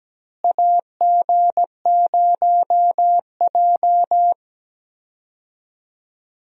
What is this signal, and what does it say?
Morse code, heard as a keyed tone, about 17 wpm, 700 Hz. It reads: AG0J
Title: USA Call Signs - 4 Characters - Repeated 2x - 17wpm